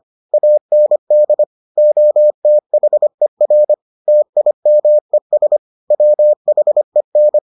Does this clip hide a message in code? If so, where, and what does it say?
Yes: Morse code from 0.3 to 7.4 s, heard as a keyed tone, about 25 wpm, 600 Hz. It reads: AND OTHER TIMES WHEN